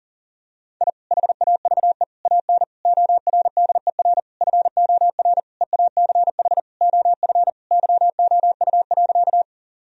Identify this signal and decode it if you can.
Morse code, heard as a keyed tone, about 40 wpm, 700 Hz. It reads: I HAVE AN ORDER FOR EACH OF YOU.
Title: Sentences from Top 1000 Words - Review Part 2 - 40wpm